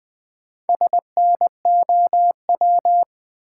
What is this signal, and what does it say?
Morse code, heard as a keyed tone, about 20 wpm, 700 Hz. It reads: SNOW
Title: Four Letter Words - 20wpm